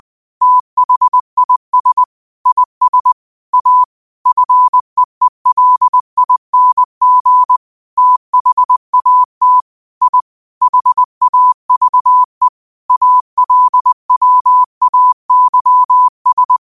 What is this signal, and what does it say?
Morse code, heard as a keyed tone, about 20 wpm, 1000 Hz. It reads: THIS IS A FEELING THAT I HAVE ALWAYS
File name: qotdfiles / 20wpm